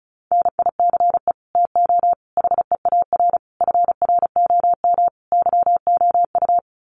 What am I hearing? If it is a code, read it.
Morse code, heard as a keyed tone, about 34 wpm, 700 Hz. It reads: NICE TO HEAR FROM YOU
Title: Common QSO Phrases 35wpm